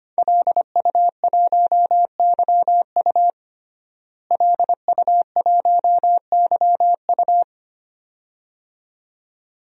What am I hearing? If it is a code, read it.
Morse code, heard as a keyed tone, about 25 wpm, 700 Hz. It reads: LU1YU LU1YU
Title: DXCC-100 Call Signs - 5 Characters - Repeated 2x - 25wpm